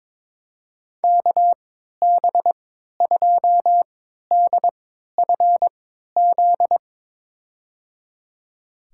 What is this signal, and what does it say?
Morse code, heard as a keyed tone, about 22 wpm, spaced as for 12 wpm, 700 Hz. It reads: KB2DFZ
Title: USA Call Signs - 6 Characters - 3x Character Spacing - 22wpm